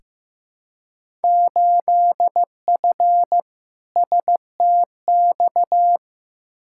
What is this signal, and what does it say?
Morse code, heard as a keyed tone, about 15 wpm, 700 Hz. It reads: 8F STX
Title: Field Day Exchanges Encore - 15wpm